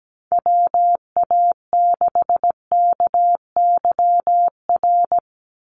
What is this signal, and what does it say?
Morse code, heard as a keyed tone, about 17 wpm, 700 Hz. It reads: WA6KYR
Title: NA SOTA Chaser Callsigns 17wpm